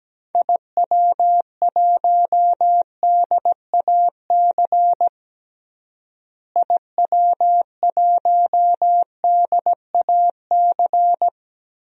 Morse code, heard as a keyed tone, about 17 wpm, 700 Hz. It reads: IW1DAC IW1DAC
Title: DXCC-100 Call Signs - 6 Characters - Repeated 2x - 17wpm